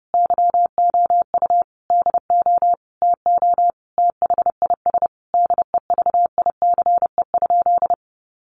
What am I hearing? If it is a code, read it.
Morse code, heard as a keyed tone, about 30 wpm, 700 Hz. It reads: YOU DO TO THIS DEVICE?
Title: Sentences from Top 1500 Words 30wpm